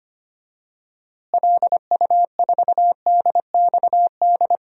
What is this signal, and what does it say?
Morse code, heard as a keyed tone, about 25 wpm, 700 Hz. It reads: LU4DXD
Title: DXCC-100 Call Signs - 6 Characters - Repeated 3x - 25wpm